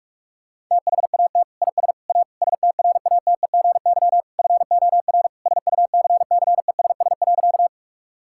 Morse code, heard as a keyed tone, about 46 wpm, 700 Hz. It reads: THAT IS A STRATEGY FOR SUCCESS.
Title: Sentences from Top 1000 Words - Review Part 5 - 45wpm